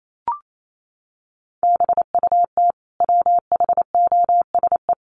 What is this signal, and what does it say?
Morse code, heard as a keyed tone, about 28 wpm, 700 Hz. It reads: BUT WHOSE
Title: Sentences from Top 1500 Words 28wpm